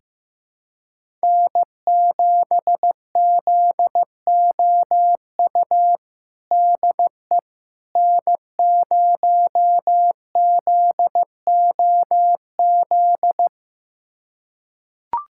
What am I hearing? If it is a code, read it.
Morse code, heard as a keyed tone, about 15 wpm, 700 Hz. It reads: N7ZOU DE N0ZOZ
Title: USA Callsign DE Callsign - 5 Characters - 15wpm